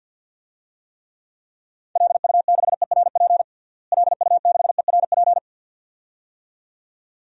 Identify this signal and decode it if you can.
Morse code, heard as a keyed tone, about 46 wpm, 700 Hz. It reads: LU6ERP LU6ERP
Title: DXCC-100 Call Signs - 6 Characters - Repeated 2x - 50wpm